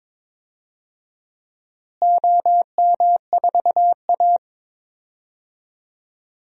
Morse code, heard as a keyed tone, about 22 wpm, 700 Hz. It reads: OM4A
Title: DXCC-100 Call Signs - 4 Characters - Repeated 2x - 22wpm